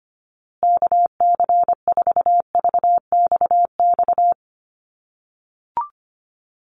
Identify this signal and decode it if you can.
Morse code, heard as a keyed tone, about 25 wpm, 700 Hz. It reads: KC4VXX
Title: USA Call Signs Repeated 2x 25wpm